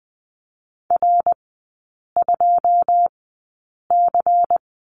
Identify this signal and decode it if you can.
Morse code, heard as a keyed tone, about 20 wpm, spaced as for 8 wpm, 700 Hz. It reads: R2C
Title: 100 Lesson 17e - Letter u - Three Characters 20wpm